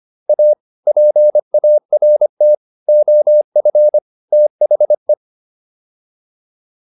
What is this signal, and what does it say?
Morse code, heard as a keyed tone, about 25 wpm, 600 Hz. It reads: A PART OF THE